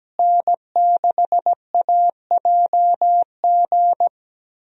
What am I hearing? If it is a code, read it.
Morse code, heard as a keyed tone, about 17 wpm, 700 Hz. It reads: N6AJG
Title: USA Call Signs - 5 Characters - Repeated 2x - 17wpm